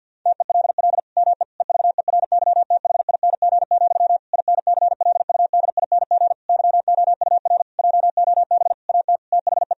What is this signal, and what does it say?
Morse code, heard as a keyed tone, about 50 wpm, 700 Hz. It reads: TELL ME EVERYTHING, INCLUDING YOUR JOB AT THE
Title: Sentences from Top 1000 Words - Review Part 4 - 50wpm